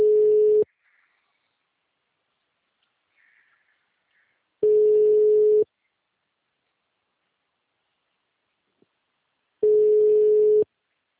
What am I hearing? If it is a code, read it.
Ringback tone (Europe, 425 Hz).
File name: out- -2024- - - .22609